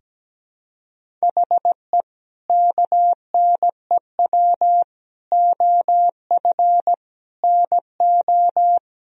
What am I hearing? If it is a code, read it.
Morse code, heard as a keyed tone, about 17 wpm, 700 Hz. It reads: HE KNEW OF NO
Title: Sentences from Top 1000 Words - Review Part 2 - 17wpm